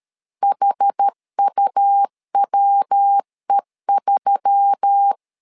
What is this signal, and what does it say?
Morse code, heard as a keyed tone, about 13 wpm, 800 Hz. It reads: HUWE3
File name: LEZIONI / 4 Q1S3 12wpm 2023